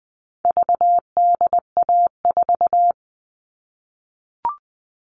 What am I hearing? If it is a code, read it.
Morse code, heard as a keyed tone, about 20 wpm, 700 Hz. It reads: VDA4
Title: All Cut Numbers - 4-Digit 20wpm